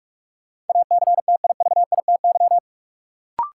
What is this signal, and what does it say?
Morse code, heard as a keyed tone, about 46 wpm, 700 Hz. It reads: ACTIVITY
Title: Sentences from Top 1000 Words - Review Part 4 - New Word Drill - 45wpm